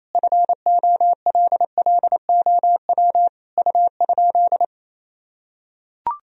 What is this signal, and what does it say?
Morse code, heard as a keyed tone, about 28 wpm, 700 Hz. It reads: FOLLOW U?
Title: Sentences from Top 200 Words and Abbreviations 28wpm